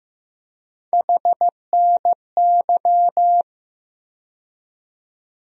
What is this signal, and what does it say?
Morse code, heard as a keyed tone, about 15 wpm, 700 Hz. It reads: HNY
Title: USA Call Sign Postfixes - 3 Letters 15wpm